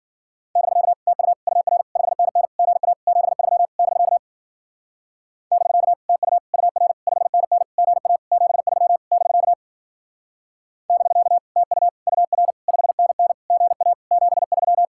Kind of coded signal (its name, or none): Morse code